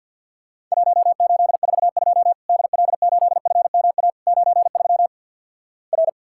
Morse code, heard as a keyed tone, about 50 wpm, 700 Hz. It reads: 1842 DL8FMA 93 R
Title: Worked All Europe DX Contest - QTC Exchange - 50wpm